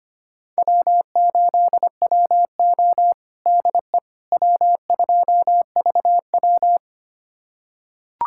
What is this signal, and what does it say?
Morse code, heard as a keyed tone, about 25 wpm, 700 Hz. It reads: W8WO DE W2VW
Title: USA Callsign DE Callsign - 4 Characters - 25wpm